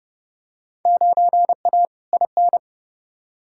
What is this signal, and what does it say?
Morse code, heard as a keyed tone, about 30 wpm, 700 Hz. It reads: 9A IN
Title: Field Day Exchanges Encore - 30wpm